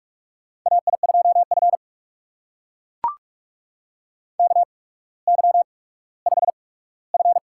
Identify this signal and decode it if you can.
Morse code, heard as a keyed tone, about 44 wpm, spaced as for 19 wpm, 700 Hz. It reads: AI2F KY5F
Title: USA Call Signs - 4 Characters - 8x Character Spacing - 45wpm